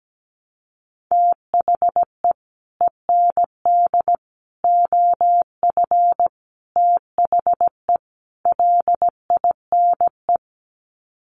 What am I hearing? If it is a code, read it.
Morse code, heard as a keyed tone, about 17 wpm, 700 Hz. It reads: THE END OF THE LINE
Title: Sets of 5 Words Encore2 17wpm